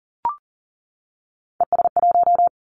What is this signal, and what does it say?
Morse code, heard as a keyed tone, about 40 wpm, 700 Hz. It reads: ES1
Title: Callsign Prefixes for Top 50 DXCC - 40wpm